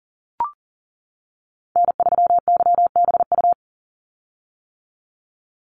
Morse code, heard as a keyed tone, about 40 wpm, 700 Hz. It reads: N3YBU
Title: USA Call Signs 40wpm